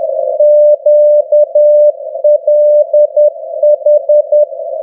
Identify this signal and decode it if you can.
Morse code, heard as a keyed tone, about 10 wpm, 600 Hz. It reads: QLH
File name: europe / f-2936-vvv de fav22-2015-01-k-rb